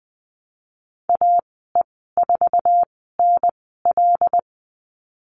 Morse code, heard as a keyed tone, about 20 wpm, spaced as for 14 wpm, 700 Hz. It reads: AE4NL
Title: 082 Lesson 14g - Number 2 - Call Signs 20wpm